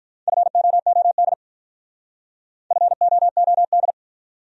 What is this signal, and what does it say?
Morse code, heard as a keyed tone, about 45 wpm, 700 Hz. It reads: FOOD FOOD